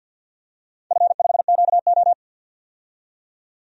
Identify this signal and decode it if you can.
Morse code, heard as a keyed tone, about 53 wpm, 700 Hz. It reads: F5QO
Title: DXCC-100 Call Signs - 4 Characters - 50wpm